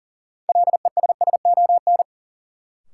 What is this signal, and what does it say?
Morse code, heard as a keyed tone, about 39 wpm, 700 Hz. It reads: LESSON
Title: Sentences from Top 1200 Words - New Word Drill - 40wpm